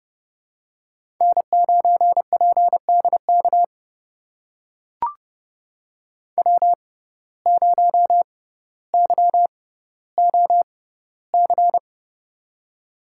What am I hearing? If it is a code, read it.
Morse code, heard as a keyed tone, about 30 wpm, spaced as for 15 wpm, 700 Hz. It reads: N9PDK W0YOC
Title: USA Call Signs - 6x Character Spacing - 30wpm